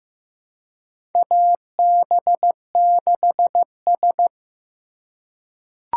Morse code, heard as a keyed tone, about 15 wpm, 700 Hz. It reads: AB6S
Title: USA Call Signs - 4 Characters - 3x Character Spacing - 15wpm